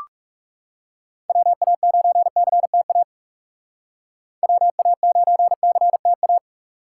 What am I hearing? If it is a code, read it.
Morse code, heard as a keyed tone, about 35 wpm, 700 Hz. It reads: WA9CTA WA9CTA
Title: Speed Racing - USA 2x3 Call Signs 30wpm